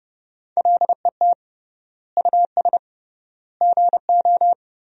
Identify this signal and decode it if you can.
Morse code, heard as a keyed tone, about 32 wpm, 700 Hz. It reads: LET US GO
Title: 3x Word-Spacing - Sentences from Top 300 Words - 30wpm